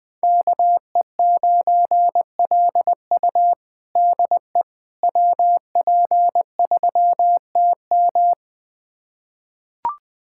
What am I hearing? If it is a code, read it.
Morse code, heard as a keyed tone, about 20 wpm, 700 Hz. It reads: KE9LU DE WP3TM
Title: USA Callsign DE Callsign - 5 Characters - 20wpm